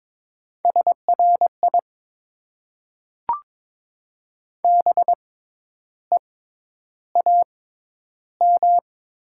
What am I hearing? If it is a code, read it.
Morse code, heard as a keyed tone, about 22 wpm, spaced as for 10 wpm, 700 Hz. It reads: SRI BEAM
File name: Top 90 QSO Elements - 6x Character Spacing - 22wpm